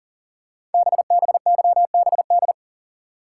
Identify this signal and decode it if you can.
Morse code, heard as a keyed tone, about 42 wpm, 700 Hz. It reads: BBYBD
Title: Code Groups - 5 Letters - 40wpm